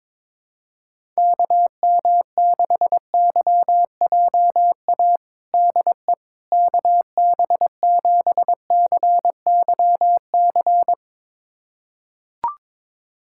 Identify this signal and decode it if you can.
Morse code, heard as a keyed tone, about 21 wpm, 700 Hz. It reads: KM6YJA DE KB7CYC
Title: USA Callsign DE Callsign - 6 Characters - 22wpm